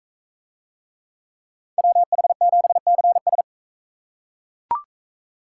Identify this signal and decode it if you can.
Morse code, heard as a keyed tone, about 39 wpm, 700 Hz. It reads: WH7CS